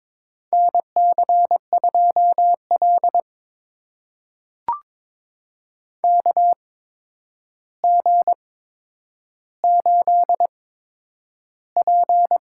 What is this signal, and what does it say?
Morse code, heard as a keyed tone, about 22 wpm, spaced as for 9 wpm, 700 Hz. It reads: NC2L KG8P
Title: USA Call Signs - 4 Characters - 8x Character Spacing - 22wpm